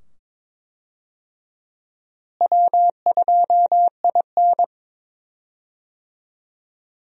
Morse code, heard as a keyed tone, about 22 wpm, 700 Hz. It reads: W2IN
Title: USA Call Signs - 4 Characters - Repeated 2x - 22wpm